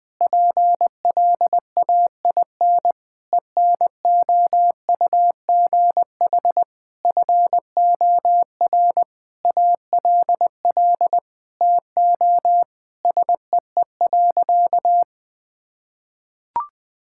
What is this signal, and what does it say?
Morse code, heard as a keyed tone, about 21 wpm, 700 Hz. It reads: PLAIN ENOUGH FOR ALL TO SEE.
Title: Sentences from Top 1000 Words - Review Part 5 - 20wpm